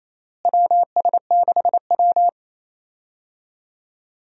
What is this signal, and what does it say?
Morse code, heard as a keyed tone, about 28 wpm, 700 Hz. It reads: WS6W
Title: USA Call Signs - 4 Characters - Repeated 3x - 28wpm